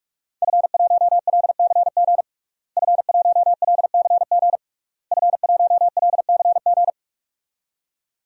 Morse code, heard as a keyed tone, about 42 wpm, 700 Hz. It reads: F1LCG F1LCG F1LCG
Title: DXCC-100 Call Signs - 5 Characters - Repeated 3x - 45wpm